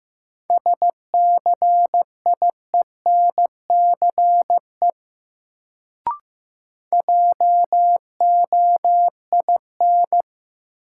Morse code, heard as a keyed tone, about 15 wpm, 700 Hz. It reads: SCIENCE JOIN
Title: Sentences from Top 1400 Words - New Word Drill Alt - 15wpm